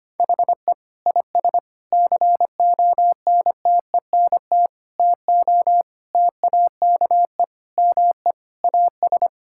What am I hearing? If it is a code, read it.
Morse code, heard as a keyed tone, about 25 wpm, 700 Hz. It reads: HE IS CONTENT TO TAKE ME AS
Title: Sentences from Top 1000 Words - Review Part 6 - 25wpm